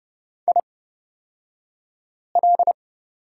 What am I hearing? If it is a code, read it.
Morse code, heard as a keyed tone, about 30 wpm, spaced as for 5 wpm, 700 Hz. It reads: IL
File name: Lesson 37d - Letter x - Two Characters 30wpm